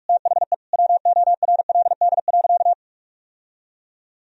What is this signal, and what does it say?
Morse code, heard as a keyed tone, about 45 wpm, 700 Hz. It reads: THE WORLD.
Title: Sentences from Top 1000 Words - Review Part 3 - 45wpm